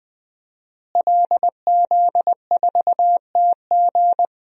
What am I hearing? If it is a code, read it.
Morse code, heard as a keyed tone, about 20 wpm, 700 Hz. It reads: LZ4TG
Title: DXCC-100 Call Signs - 5 Characters - Repeated 2x - 20wpm